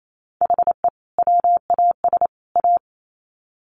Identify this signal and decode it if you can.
Morse code, heard as a keyed tone, about 28 wpm, 700 Hz. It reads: HE WAS A
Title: Sets of 3 Words 28wpm